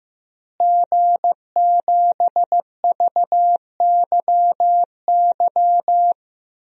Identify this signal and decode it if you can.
Morse code, heard as a keyed tone, about 16 wpm, 700 Hz. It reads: G7VYY
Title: DXCC-100 Call Signs - 5 Characters - Repeated 2x - 15wpm